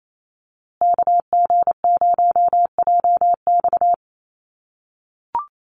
Morse code, heard as a keyed tone, about 28 wpm, 700 Hz. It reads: KG0JX
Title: USA Call Signs Repeated 3x 28wpm